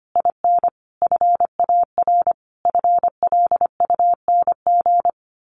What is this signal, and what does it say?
Morse code, heard as a keyed tone, about 25 wpm, 700 Hz. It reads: IN FAR FLUNG
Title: A Sound Of Thunder 25wpm